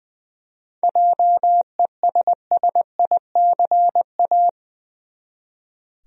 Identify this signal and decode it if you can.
Morse code, heard as a keyed tone, about 20 wpm, 700 Hz. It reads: JESSICA
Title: USA Common Names - 20wpm